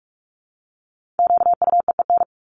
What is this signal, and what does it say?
Morse code, heard as a keyed tone, about 44 wpm, 700 Hz. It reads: QUEEN
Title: Top 1751-2000 Words 45wpm